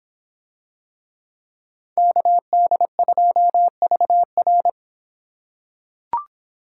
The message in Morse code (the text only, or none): KD2VR